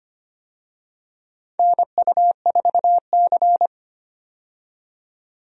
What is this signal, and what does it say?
Morse code, heard as a keyed tone, about 25 wpm, 700 Hz. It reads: NU4C
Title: USA Call Signs - 4 Characters - 25wpm